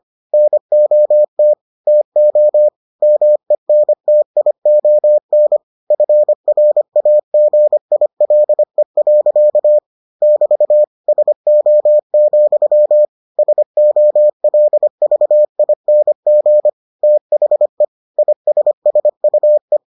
Morse code, heard as a keyed tone, about 25 wpm, 600 Hz. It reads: NOT TO MENTION FRAGILE. <BT> SO, SOLVING THE ISSUE